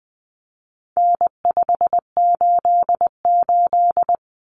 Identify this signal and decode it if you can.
Morse code, heard as a keyed tone, about 20 wpm, 700 Hz. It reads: N588
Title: Common Cut Numbers - 4-Digit 20wpm